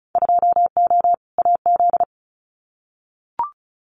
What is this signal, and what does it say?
Morse code, heard as a keyed tone, about 34 wpm, 700 Hz. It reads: 2O AZ
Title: Winter Field Day Exchanges 35wpm